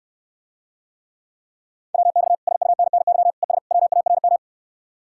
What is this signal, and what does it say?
Morse code, heard as a keyed tone, about 69 wpm, 700 Hz. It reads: WX SUNNY ES WARM